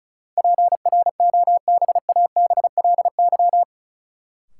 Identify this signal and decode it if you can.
Morse code, heard as a keyed tone, about 36 wpm, 700 Hz. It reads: PROBABLY
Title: Sentences from Top 700 Words - New Word Drill - 35wpm